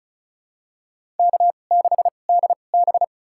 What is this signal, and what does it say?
Morse code, heard as a keyed tone, about 36 wpm, spaced as for 25 wpm, 690 Hz. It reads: K6DB
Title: USA Call Signs - 4 Characters - 2x Character Spacing - 35wpm